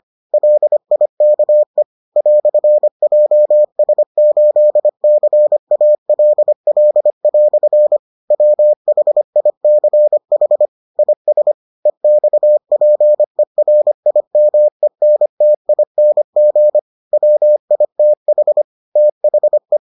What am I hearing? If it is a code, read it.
Morse code, heard as a keyed tone, about 25 wpm, 600 Hz. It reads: LIKE "JS8CALL" WHICH IS EXPERIMENTING WITH THE